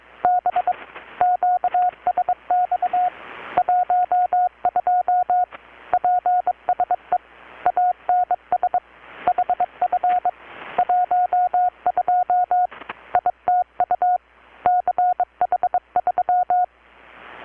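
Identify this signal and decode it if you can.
Morse code, heard as a keyed tone, about 22 wpm, 720 Hz. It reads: B QSX 12 PSE ANS HF 12 ITU CH3
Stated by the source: Morse from KPH's Night of Nights XX maritime radiotelegraphy event, received in Chiapas, Mexico about 3,800 km away on a Kenwood R-600 with a grounded 10 m long-wire antenna.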